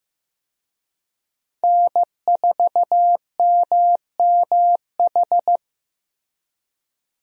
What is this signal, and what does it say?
Morse code, heard as a keyed tone, about 15 wpm, 700 Hz. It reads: N4MMH
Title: USA Call Signs - 5 Characters - Repeated 3x - 15wpm